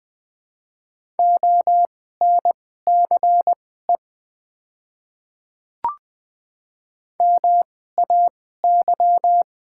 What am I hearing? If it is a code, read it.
Morse code, heard as a keyed tone, about 20 wpm, spaced as for 14 wpm, 700 Hz. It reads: ONCE MAY
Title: 2x Character Spacing - Top 300 Words - 20wpm